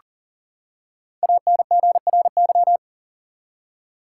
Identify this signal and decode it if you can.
Morse code, heard as a keyed tone, about 40 wpm, 700 Hz. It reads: ANGRY
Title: Five Letter Words - 40wpm